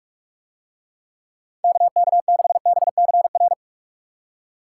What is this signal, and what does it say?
Morse code, heard as a keyed tone, about 46 wpm, 700 Hz. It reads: KK6BCR